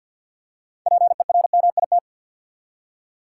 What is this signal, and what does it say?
Morse code, heard as a keyed tone, about 51 wpm, 700 Hz. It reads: PERMIT
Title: Six Letter Words - 50wpm